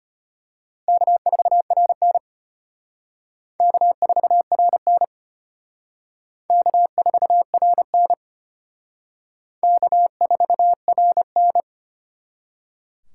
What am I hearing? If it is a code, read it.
Morse code, heard as a keyed tone, about 32 wpm, 700 Hz. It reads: K4RN K4RN K4RN K4RN